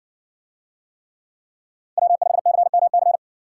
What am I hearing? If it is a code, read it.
Morse code, heard as a keyed tone, about 59 wpm, 700 Hz. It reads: W5CMC